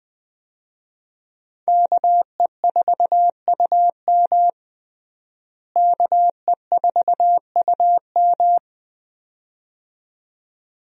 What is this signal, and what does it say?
Morse code, heard as a keyed tone, about 20 wpm, 700 Hz. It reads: KE4UM KE4UM
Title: USA Call Signs - 5 Characters - Repeated 2x - 20wpm